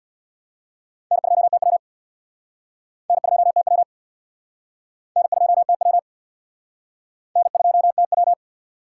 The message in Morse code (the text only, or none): N2TW N2TW N2TW N2TW